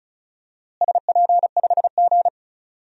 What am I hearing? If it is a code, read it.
Morse code, heard as a keyed tone, about 36 wpm, 700 Hz. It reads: SP5G
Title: DXCC-100 Call Signs - 4 Characters - Repeated 3x - 35wpm